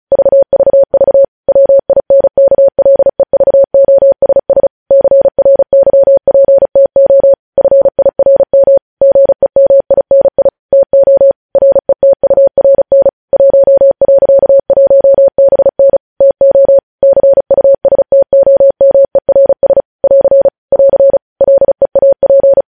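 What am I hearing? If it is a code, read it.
Morse code, heard as a keyed tone, about 36 wpm, 550 Hz. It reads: VVV WINKLEVOSS CRYPTO FIRM GEMINI TO RETURN 1.1BN TO CUSTOMERS <AR> <AR> LEAP
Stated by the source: Morse code news headlines sent at 35 wpm.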